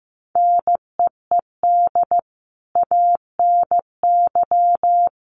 Morse code, heard as a keyed tone, about 15 wpm, 700 Hz. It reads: NEED ANY
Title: Sentences from Top 1200 Words 15wpm